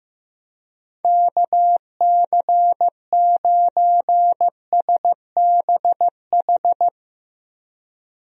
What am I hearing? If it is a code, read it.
Morse code, heard as a keyed tone, about 16 wpm, 700 Hz. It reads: KC9SBH
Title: USA Call Signs - 6 Characters - 15wpm